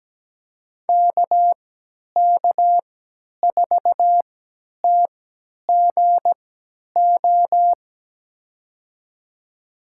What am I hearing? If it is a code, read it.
Morse code, heard as a keyed tone, about 17 wpm, spaced as for 10 wpm, 700 Hz. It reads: KK4TGO
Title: USA Call Signs - 6 Characters - 3x Character Spacing - 17wpm